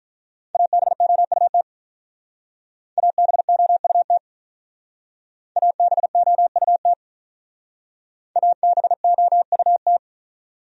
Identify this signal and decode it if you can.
Morse code, heard as a keyed tone, about 43 wpm, 700 Hz. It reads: ABOUT ABOUT ABOUT ABOUT
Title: Speed-Racing Top 100 Words 35wpm